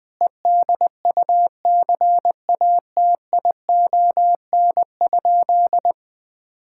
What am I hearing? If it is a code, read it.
Morse code, heard as a keyed tone, about 20 wpm, 700 Hz. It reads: EDUCATION?
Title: Sentences from Top 1000 Words - Review Part 6 - 20wpm